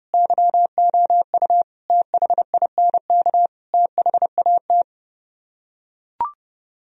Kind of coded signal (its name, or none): Morse code